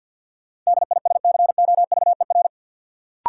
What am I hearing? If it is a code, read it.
Morse code, heard as a keyed tone, about 50 wpm, 700 Hz. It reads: DISCOVER
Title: Sentences from Top 1100 Words - New Word Drill Alt - 50wpm